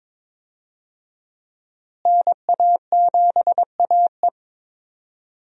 Morse code, heard as a keyed tone, about 22 wpm, 700 Hz. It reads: NA7AE
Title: USA Call Signs - 5 Characters - Repeated 2x - 22wpm